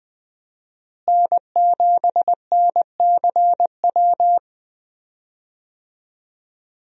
Morse code, heard as a keyed tone, about 21 wpm, 700 Hz. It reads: N7NCW
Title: USA Call Signs - 5 Characters - Repeated 2x - 20wpm